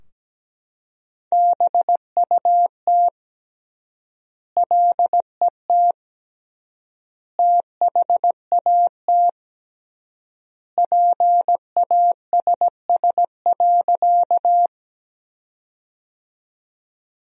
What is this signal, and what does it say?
Morse code, heard as a keyed tone, about 17 wpm, 700 Hz. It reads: BUT LET THAT PASS.
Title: 3x Word-Spacing - Sentences from Top 400 Words - 17wpm